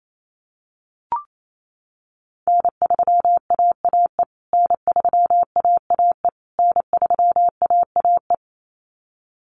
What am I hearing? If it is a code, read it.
Morse code, heard as a keyed tone, about 28 wpm, 700 Hz. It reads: N3AAE N3AAE N3AAE
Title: USA Call Signs Repeated 3x 28wpm